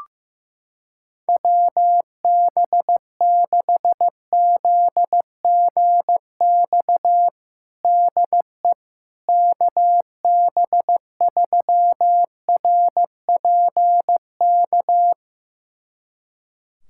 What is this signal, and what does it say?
Morse code, heard as a keyed tone, about 15 wpm, 700 Hz. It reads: WB6ZGX DE KB3RPK
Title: USA Callsign DE Callsign 15wpm